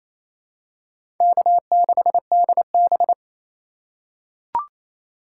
Morse code, heard as a keyed tone, about 28 wpm, 700 Hz. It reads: K6DB
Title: USA Call Signs - 4 Characters - 2x Character Spacing - 28wpm